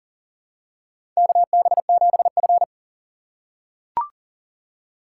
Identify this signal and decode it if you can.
Morse code, heard as a keyed tone, about 40 wpm, 700 Hz. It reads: KB7F
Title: USA Call Signs - 4 Characters - 4x Character Spacing - 40wpm